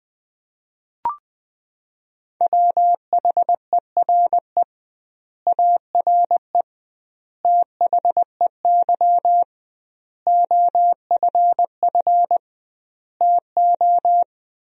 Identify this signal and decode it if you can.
Morse code, heard as a keyed tone, about 21 wpm, 700 Hz. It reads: WHERE ARE THEY OFF TO
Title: 2x Word-Spacing - Sentences from Top 200 Words - 20wpm